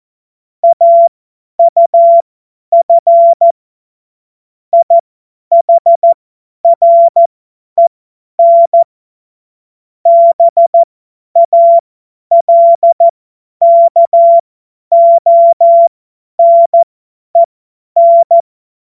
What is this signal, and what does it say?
Morse code, heard as a keyed tone, about 14 wpm, spaced as for 10 wpm, 680 Hz. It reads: AUF IHREN BALKONEN
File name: Morsekurs / Witze / Witz 22. Juni 2021, 50 F70